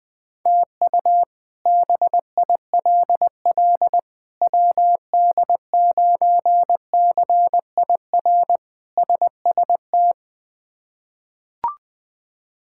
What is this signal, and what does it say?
Morse code, heard as a keyed tone, about 20 wpm, 700 Hz. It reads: TU BILL WD9CIR SST
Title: K1USN SST QSO Practice - 20wpm